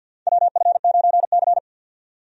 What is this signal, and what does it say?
Morse code, heard as a keyed tone, about 52 wpm, 700 Hz. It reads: WF9C
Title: USA Call Signs - 4 Characters - 50wpm